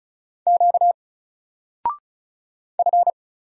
Morse code, heard as a keyed tone, about 36 wpm, 700 Hz. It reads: Q F
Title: Single Character (Letter:Number:Punctuation:Prosign) 35wpm